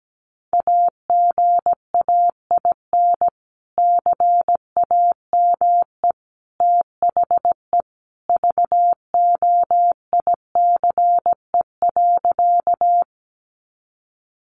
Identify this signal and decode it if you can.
Morse code, heard as a keyed tone, about 17 wpm, 700 Hz. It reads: AGAIN CAME THE VOICE.
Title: Sentences from Top 500 Words 17wpm